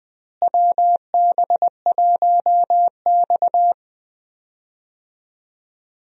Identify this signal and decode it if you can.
Morse code, heard as a keyed tone, about 20 wpm, 700 Hz. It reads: WB1X
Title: USA Call Signs - 4 Characters - Repeated 2x - 20wpm